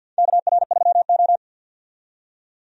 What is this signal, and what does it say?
Morse code, heard as a keyed tone, about 49 wpm, 700 Hz. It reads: KR3O
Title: USA Call Signs - 4 Characters - 50wpm